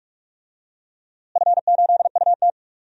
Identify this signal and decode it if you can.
Morse code, heard as a keyed tone, about 43 wpm, 700 Hz. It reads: F8UT
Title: DXCC-100 Call Signs - 4 Characters - Repeated 3x - 45wpm